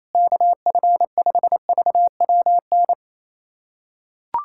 Morse code, heard as a keyed tone, about 28 wpm, 700 Hz. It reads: KF5VWN
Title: USA Callsign DE Callsign - 6 Characters - 28wpm